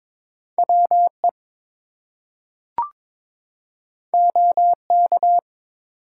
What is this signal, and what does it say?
Morse code, heard as a keyed tone, about 22 wpm, 700 Hz. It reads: WE OK
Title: Two Letter Words - 22wpm